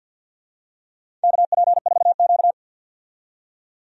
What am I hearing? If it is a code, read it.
Morse code, heard as a keyed tone, about 52 wpm, 700 Hz. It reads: KP4Q
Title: DXCC-100 Call Signs - 4 Characters - 50wpm